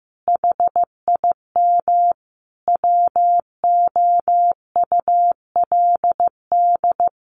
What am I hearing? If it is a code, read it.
Morse code, heard as a keyed tone, about 15 wpm, 700 Hz. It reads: HIM WOULD
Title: Sentences from Top 1400 Words 15wpm